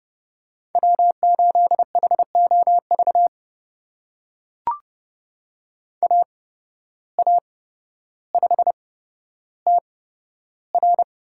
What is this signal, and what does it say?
Morse code, heard as a keyed tone, about 30 wpm, spaced as for 13 wpm, 700 Hz. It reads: W8HOV AA5TR
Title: USA Call Signs - 5 Characters - 8x Character Spacing - 30wpm